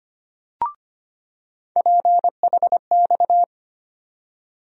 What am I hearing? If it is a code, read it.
Morse code, heard as a keyed tone, about 25 wpm, 700 Hz. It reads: PHX
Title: USA Call Sign Postfixes - 3 Letters 25wpm